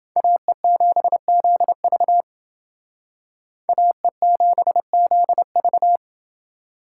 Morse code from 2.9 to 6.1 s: AE7ZV

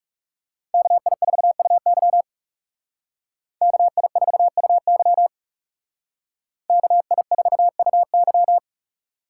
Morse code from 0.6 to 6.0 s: KI4UY KI4UY

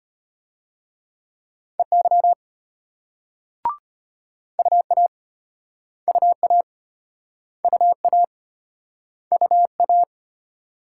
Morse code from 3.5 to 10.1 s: UA UA UA UA

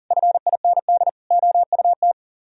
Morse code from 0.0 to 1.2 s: FIND

